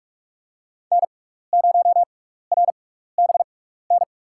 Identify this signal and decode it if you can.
Morse code, heard as a keyed tone, about 48 wpm, spaced as for 16 wpm, 700 Hz. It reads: N0RBN
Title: USA Call Signs - 5 Characters - 6x Character Spacing - 45wpm